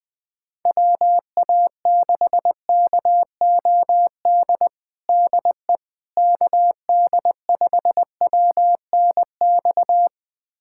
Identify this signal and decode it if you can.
Morse code, heard as a keyed tone, about 20 wpm, 700 Hz. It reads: WA6KOD DE KD5WNX
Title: USA Callsign DE Callsign - 6 Characters - 20wpm